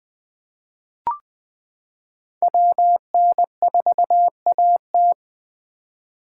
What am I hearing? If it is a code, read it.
Morse code, heard as a keyed tone, about 20 wpm, 700 Hz. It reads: WN4AT
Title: 2023 SOTA Chaser Call Signs - 20wpm